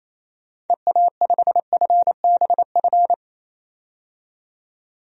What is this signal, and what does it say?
Morse code, heard as a keyed tone, about 28 wpm, 700 Hz. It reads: EA5FBF
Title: DXCC-100 Call Signs - 6 Characters - 28wpm